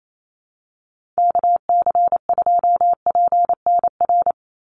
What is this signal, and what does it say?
Morse code, heard as a keyed tone, about 28 wpm, 700 Hz. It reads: KC2PNR
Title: USA Call Signs 28wpm